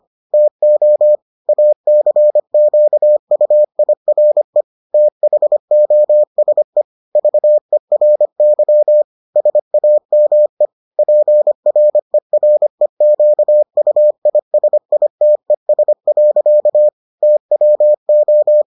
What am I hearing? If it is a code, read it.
Morse code, heard as a keyed tone, about 25 wpm, 600 Hz. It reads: TO ACQUIRE THOSE VERY SAME PREREQUISITES. TWO